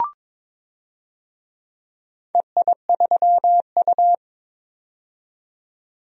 Morse code, heard as a keyed tone, about 22 wpm, 700 Hz. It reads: EI3U
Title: DXCC-100 Call Signs - 4 Characters - 22wpm